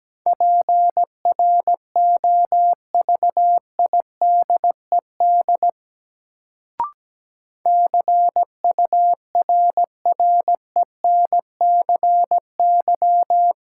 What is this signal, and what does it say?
Morse code, heard as a keyed tone, about 17 wpm, 700 Hz. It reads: PROVIDED CURRENCY
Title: Sentences from Top 1500 Words - New Word Drill Alt - 17wpm